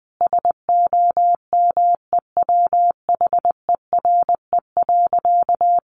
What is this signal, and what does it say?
Morse code, heard as a keyed tone, about 19 wpm, 700 Hz. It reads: SOMEWHERE.
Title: Sentences from Top 1400 Words 20wpm